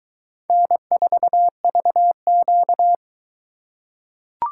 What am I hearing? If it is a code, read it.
Morse code, heard as a keyed tone, about 21 wpm, 700 Hz. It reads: N4VQ